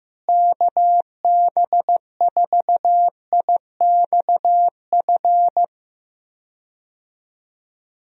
Morse code, heard as a keyed tone, about 15 wpm, 700 Hz. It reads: KB4IXF
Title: USA Call Signs - 6 Characters - Repeated 3x - 15wpm